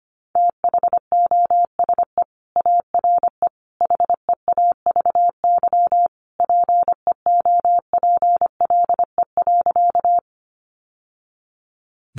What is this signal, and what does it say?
Morse code, heard as a keyed tone, about 25 wpm, 700 Hz. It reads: THOSE ARE HEAVY PEOPLE.
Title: Sentences from Top 1100 Words 25wpm